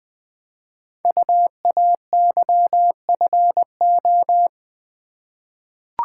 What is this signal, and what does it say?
Morse code, heard as a keyed tone, about 20 wpm, 700 Hz. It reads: UAYFO
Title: Code Groups - 5 Letters - 20wpm